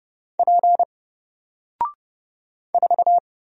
Morse code, heard as a keyed tone, about 30 wpm, 700 Hz. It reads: P 4
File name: Single-Letter-Number-30wpm